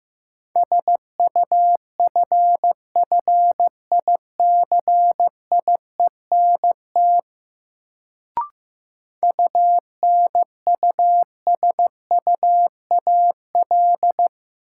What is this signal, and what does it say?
Morse code, heard as a keyed tone, about 15 wpm, 700 Hz. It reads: SUFFICIENT UNUSUAL
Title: Sentences from Top 1400 Words - New Word Drill Alt - 15wpm